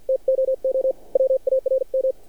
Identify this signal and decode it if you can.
Morse code, heard as a keyed tone, about 50 wpm, spaced as for 43 wpm, 530 Hz. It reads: TOO WARM
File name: archived / CRSnet